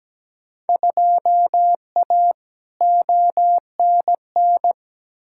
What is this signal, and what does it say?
Morse code, heard as a keyed tone, about 17 wpm, 700 Hz. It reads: 2A ONN
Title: Field Day Exchanges Encore - 17wpm